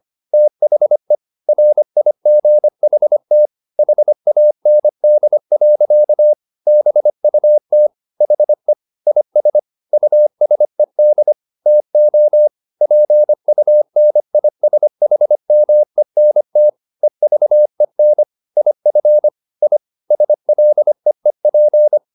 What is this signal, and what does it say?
Morse code, heard as a keyed tone, about 25 wpm, 600 Hz. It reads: THE RIGHT HAND. BUT HE IS USED TO PUNISHMENT EVEN IF I SLEEP